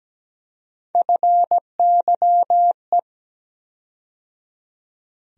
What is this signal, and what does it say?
Morse code, heard as a keyed tone, about 17 wpm, 700 Hz. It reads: FYE
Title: USA Call Sign Postfixes - 3 Letters 17wpm